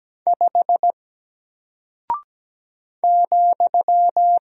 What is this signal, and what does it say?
Morse code, heard as a keyed tone, about 17 wpm, 700 Hz. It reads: 5 ,
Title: Single Character (Letter:Number:Punctuation:Prosign) Alt 17wpm